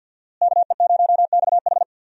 Morse code, heard as a keyed tone, about 50 wpm, 700 Hz. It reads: KE0XH
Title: USA Call Signs - 5 Characters - Repeated 2x - 50wpm